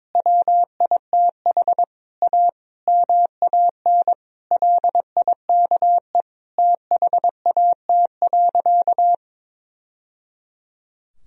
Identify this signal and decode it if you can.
Morse code, heard as a keyed tone, about 22 wpm, 700 Hz. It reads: WITH A MAN LIKE THAT.
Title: Sentences from Top 1000 Words - Review Part 4 - 22wpm